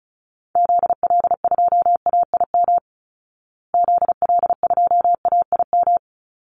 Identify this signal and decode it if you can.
Morse code, heard as a keyed tone, about 34 wpm, 700 Hz. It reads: ZL2AIM ZL2AIM
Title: Call Signs Repeated 3x - 35wpm